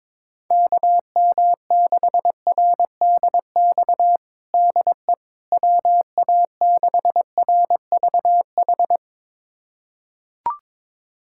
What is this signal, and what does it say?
Morse code, heard as a keyed tone, about 22 wpm, 700 Hz. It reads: KM6RDX DE WA6RVH
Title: USA Callsign DE Callsign - 6 Characters - 22wpm